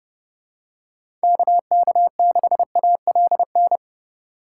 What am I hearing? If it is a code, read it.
Morse code, heard as a keyed tone, about 30 wpm, 700 Hz. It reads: KK6ALN
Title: USA Call Signs - 6 Characters - 4x Character Spacing - 30wpm